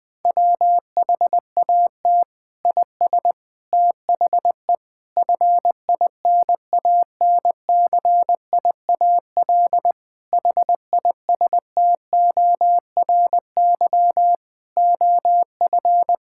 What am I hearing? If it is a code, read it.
Morse code, heard as a keyed tone, about 20 wpm, 700 Hz. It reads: WHAT IS THE FINANCIAL HISTORY OF
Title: Sentences from Top 1000 Words - Review Part 5 - 20wpm